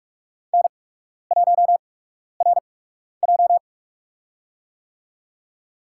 Morse code, heard as a keyed tone, about 46 wpm, spaced as for 12 wpm, 700 Hz. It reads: N1RJ
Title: USA Call Signs - 4 Characters - 8x Character Spacing - 45wpm